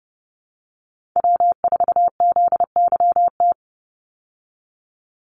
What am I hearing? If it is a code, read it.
Morse code, heard as a keyed tone, about 30 wpm, 700 Hz. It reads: W4ZYT
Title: Call Signs Repeated 3x - 30wpm